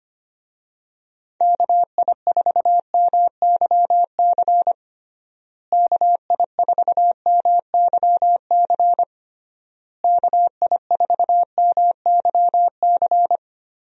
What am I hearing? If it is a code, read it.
Morse code, heard as a keyed tone, about 25 wpm, 700 Hz. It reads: KI4MYC KI4MYC KI4MYC
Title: USA Call Signs - 6 Characters - Repeated 3x - 25wpm